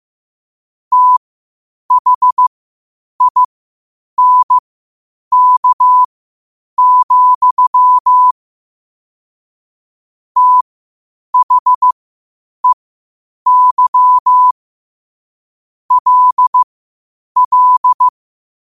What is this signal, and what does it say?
Morse code, heard as a keyed tone, about 15 wpm, spaced as for 8 wpm, 1000 Hz. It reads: THINK, THEY LL
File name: qotdfiles / 8wpm